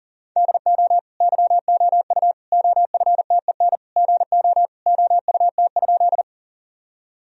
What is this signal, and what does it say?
Morse code, heard as a keyed tone, about 40 wpm, 700 Hz. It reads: DO YOU OFTEN GO OUT?
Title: Sentences from Top 1000 Words - Review Part 1 - 40wpm